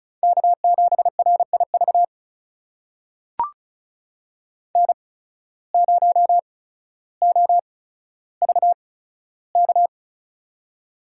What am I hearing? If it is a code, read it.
Morse code, heard as a keyed tone, about 36 wpm, spaced as for 15 wpm, 700 Hz. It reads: K7RIV N0OVK
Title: USA Call Signs - 5 Characters - 8x Character Spacing - 35wpm